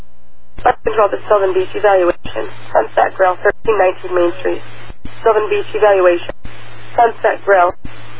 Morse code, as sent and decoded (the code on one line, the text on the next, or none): none